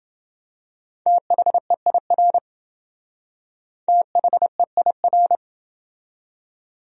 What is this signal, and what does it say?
Morse code, heard as a keyed tone, about 28 wpm, 700 Hz. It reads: THEIR THEIR